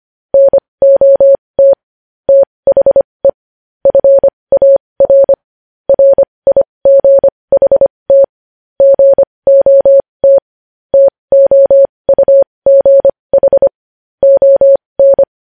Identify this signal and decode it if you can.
Morse code, heard as a keyed tone, about 25 wpm, spaced as for 20 wpm, 550 Hz. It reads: NOT THE FAR RIGHT GOT TOUGH ON
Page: news-headlines-in-morse-code-at-20-wpm